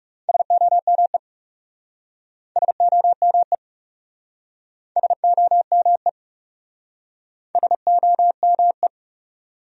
Morse code, heard as a keyed tone, about 37 wpm, 700 Hz. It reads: SOME SOME SOME SOME
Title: Speed-Racing Top 100 Words 30wpm